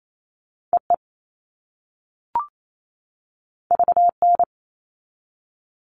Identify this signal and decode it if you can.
Morse code, heard as a keyed tone, about 28 wpm, 700 Hz. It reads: EE VN
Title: All Cut Numbers - 2-Digit 28wpm